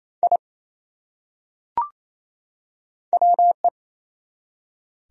Morse code, heard as a keyed tone, about 28 wpm, 700 Hz. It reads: I WE
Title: Top 100 Words 28wpm